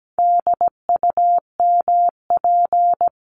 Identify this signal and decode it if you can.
Morse code, heard as a keyed tone, about 17 wpm, 700 Hz. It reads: DUMP
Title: Top 1001-1250 Words 17wpm